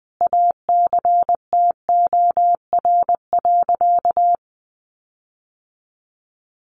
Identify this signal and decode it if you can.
Morse code, heard as a keyed tone, about 20 wpm, 700 Hz. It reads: ACTOR.
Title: Sentences from Top 1500 Words 20wpm